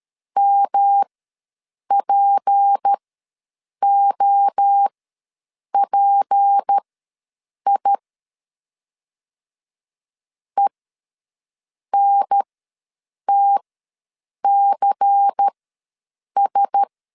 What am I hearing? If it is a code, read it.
Morse code, heard as a keyed tone, about 13 wpm, spaced as for 7 wpm, 800 Hz. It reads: MPOPI ENTCS
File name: LEZIONI / 6 PKX0 / 6 P ws 12wpm 3117